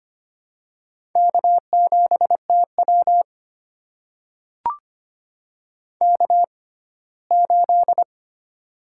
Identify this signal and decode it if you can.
Morse code, heard as a keyed tone, about 25 wpm, spaced as for 17 wpm, 700 Hz. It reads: K7TW K8
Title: USA Call Signs - 4 Characters - 6x Character Spacing - 25wpm